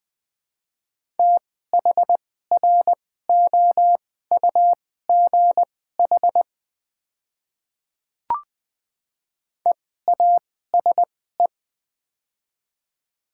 Morse code, heard as a keyed tone, about 20 wpm, spaced as for 14 wpm, 700 Hz. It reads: THROUGH EASE
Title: 2x Character Spacing - Top 300 Words - 20wpm